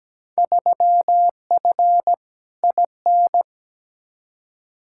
Morse code, heard as a keyed tone, about 17 wpm, 700 Hz. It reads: 3F IN
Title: Field Day Exchanges - 17wpm